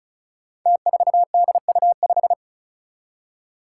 Morse code, heard as a keyed tone, about 36 wpm, 700 Hz. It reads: T4DU5
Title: Five Challenging Letter-Number Combinations 35wpm